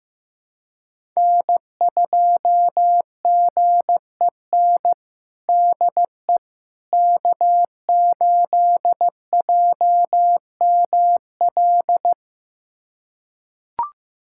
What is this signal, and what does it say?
Morse code, heard as a keyed tone, about 15 wpm, 700 Hz. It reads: N2GEN DE K8JML
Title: USA Callsign DE Callsign - 5 Characters - 15wpm